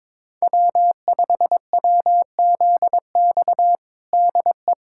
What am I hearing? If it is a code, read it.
Morse code, heard as a keyed tone, about 22 wpm, 700 Hz. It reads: W5WZX DE
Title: USA Callsign DE Callsign 22wpm